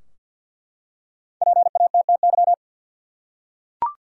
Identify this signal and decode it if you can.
Morse code, heard as a keyed tone, about 50 wpm, 700 Hz. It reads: PATTY
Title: USA Common Names - 50wpm